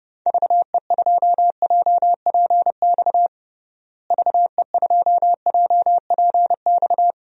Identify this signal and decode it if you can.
Morse code, heard as a keyed tone, about 32 wpm, 700 Hz. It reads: VE2JPX VE2JPX
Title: DXCC-100 Call Signs - 6 Characters - Repeated 2x - 30wpm